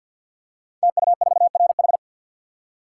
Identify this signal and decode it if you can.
Morse code, heard as a keyed tone, about 50 wpm, 700 Hz. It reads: TU4RH
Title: Code Groups - 5 Letter:Numbers - 50wpm